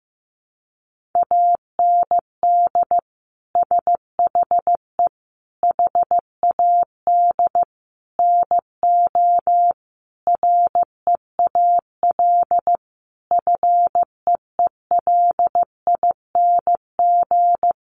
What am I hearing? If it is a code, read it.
Morse code, heard as a keyed tone, about 15 wpm, 700 Hz. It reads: AND SHE HAD NO REAL FEELING
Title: Sentences from Top 1200 Words 15wpm